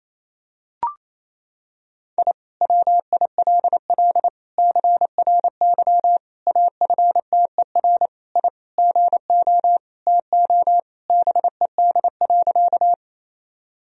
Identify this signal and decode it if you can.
Morse code, heard as a keyed tone, about 28 wpm, 700 Hz. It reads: I WILL CRY AFTER I GO TO BED.
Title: Sentences from Top 1000 Words - Review Part 2 - 28wpm